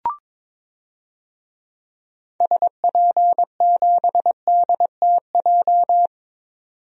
Morse code, heard as a keyed tone, about 22 wpm, 700 Hz. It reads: SP7DTJ
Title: DXCC-100 Call Signs - 6 Characters - 22wpm